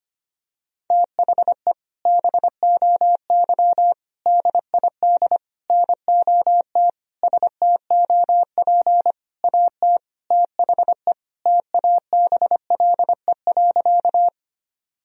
Morse code, heard as a keyed tone, about 25 wpm, 700 Hz. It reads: THE BOY DID NOT STOP AT THE TABLE.
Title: Sentences from Top 1000 Words - Review Part 2 - 25wpm